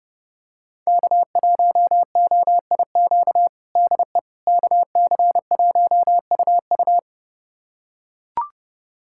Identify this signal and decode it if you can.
Morse code, heard as a keyed tone, about 30 wpm, 700 Hz. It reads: K1OIQ DE KC1UU
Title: USA Callsign DE Callsign - 5 Characters - 30wpm